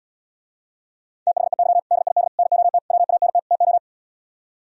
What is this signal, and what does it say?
Morse code, heard as a keyed tone, about 78 wpm, 700 Hz. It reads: THEY DID NOT WANT TO